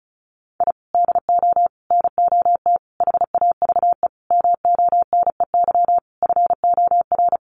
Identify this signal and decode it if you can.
Morse code, heard as a keyed tone, about 34 wpm, 700 Hz. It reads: I DO NOT HAVE MONEY FOR
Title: Sentences from Top 1400 Words 35wpm